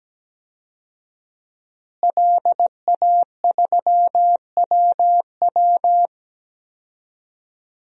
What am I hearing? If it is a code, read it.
Morse code, heard as a keyed tone, about 17 wpm, 700 Hz. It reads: LA3WW
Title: DXCC-100 Call Signs - 5 Characters - Repeated 2x - 17wpm